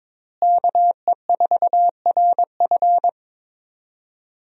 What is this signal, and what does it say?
Morse code, heard as a keyed tone, about 22 wpm, 700 Hz. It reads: KE4RF
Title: USA Call Signs - 5 Characters - Repeated 3x - 22wpm